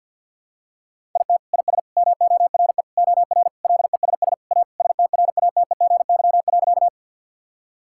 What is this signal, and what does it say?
Morse code, heard as a keyed tone, about 52 wpm, 700 Hz. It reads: IT IS MORE OR LESS A STRATEGY.
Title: Sentences from Top 1000 Words - Review Part 5 - 50wpm